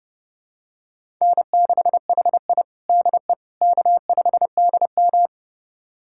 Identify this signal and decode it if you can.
Morse code, heard as a keyed tone, about 30 wpm, 700 Hz. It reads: N6HI DE K5DM
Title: USA Callsign DE Callsign - 4 Characters - 30wpm